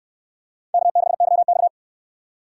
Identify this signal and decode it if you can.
Morse code, heard as a keyed tone, about 69 wpm, 700 Hz. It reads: K6YX